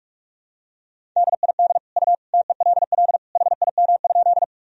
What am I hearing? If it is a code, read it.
Morse code, heard as a keyed tone, about 46 wpm, 700 Hz. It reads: DID U TELL HIM?
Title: Sentences from Top 200 Words and Abbreviations 45wpm